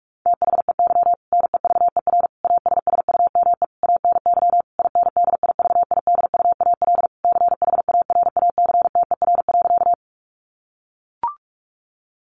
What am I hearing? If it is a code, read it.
Morse code, heard as a keyed tone, about 44 wpm, 700 Hz. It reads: THEY NEVER ASSUME ANY INDIVIDUAL CHARACTER.
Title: Sentences from Top 1500 Words 45wpm